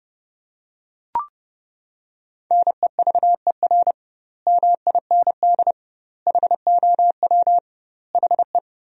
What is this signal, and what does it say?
Morse code, heard as a keyed tone, about 30 wpm, 700 Hz. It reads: NEVER MIND HOW HE
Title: 2x Word-Spacing - Sentences from Top 500 Words - 30wpm